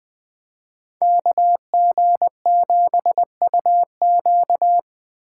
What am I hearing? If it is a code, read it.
Morse code, heard as a keyed tone, about 20 wpm, 700 Hz. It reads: KG7UQ
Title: Speed Racing - USA 2x2 Call Signs 17wpm